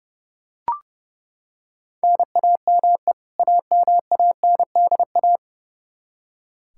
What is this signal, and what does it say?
Morse code, heard as a keyed tone, about 30 wpm, 700 Hz. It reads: NAME AMANDA
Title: Practice QSO - Exchange 1 - Beginning - 30wpm